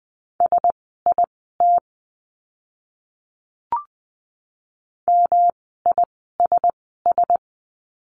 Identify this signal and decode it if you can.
Morse code, heard as a keyed tone, about 20 wpm, spaced as for 14 wpm, 700 Hz. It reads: SIT MISS
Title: 108 Lesson 18f - Letter m - Words 20wpm